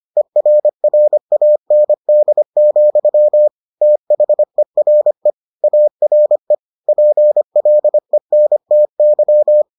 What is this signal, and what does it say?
Morse code, heard as a keyed tone, about 25 wpm, 600 Hz. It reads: ERRAND, THERE ARE PLENTY